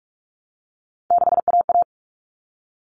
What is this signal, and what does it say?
Morse code, heard as a keyed tone, about 44 wpm, 700 Hz. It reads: 6AA
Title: All Cut Numbers - 3-Digit 45wpm